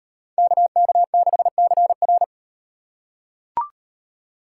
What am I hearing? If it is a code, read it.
Morse code, heard as a keyed tone, about 38 wpm, 700 Hz. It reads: KK6CR